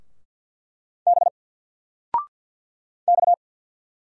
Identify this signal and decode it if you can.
Morse code, heard as a keyed tone, about 50 wpm, 700 Hz. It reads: B X
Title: Single Letter-Number 50wpm